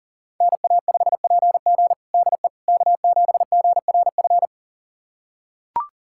Morse code, heard as a keyed tone, about 40 wpm, 700 Hz. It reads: NA5PG DE K7GRF
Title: USA Callsign DE Callsign - 5 Characters - 40wpm